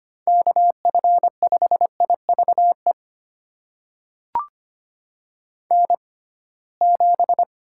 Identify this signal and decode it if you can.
Morse code, heard as a keyed tone, about 25 wpm, spaced as for 19 wpm, 700 Hz. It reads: KF5IVE N7
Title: USA Call Signs - 6x Character Spacing - 25wpm